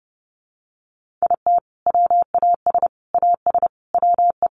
Morse code, heard as a keyed tone, about 28 wpm, 700 Hz. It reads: IT WAS AS WE
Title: Sentences from Top 1500 Words 30wpm